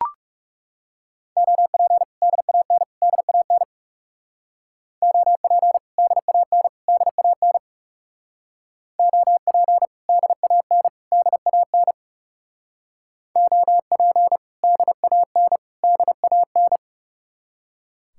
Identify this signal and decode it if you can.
Morse code, heard as a keyed tone, about 38 wpm, 700 Hz. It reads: OP DAN DAN OP DAN DAN OP DAN DAN OP DAN DAN